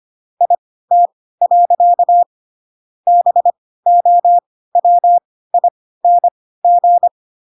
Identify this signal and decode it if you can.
Morse code, heard as a keyed tone, about 25 wpm, spaced as for 16 wpm, 700 Hz. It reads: IT. BOWING